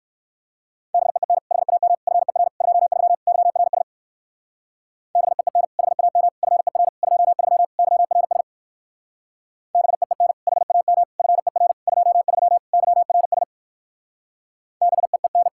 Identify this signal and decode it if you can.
Morse code, heard as a keyed tone, about 59 wpm, 700 Hz. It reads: BEEN HAM FER 23 YRS BEEN HAM FER 23 YRS BEEN HAM FER 23 YRS BEEN